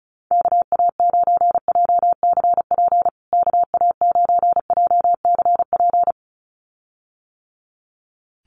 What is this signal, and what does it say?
Morse code, heard as a keyed tone, about 34 wpm, 700 Hz. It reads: KA9JCP KA9JCP
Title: USA Call Signs Repeated 2x 35wpm